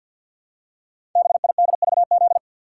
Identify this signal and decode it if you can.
Morse code, heard as a keyed tone, about 48 wpm, 700 Hz. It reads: BIDVZ
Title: Code Groups - 5 Letter:Numbers - 50wpm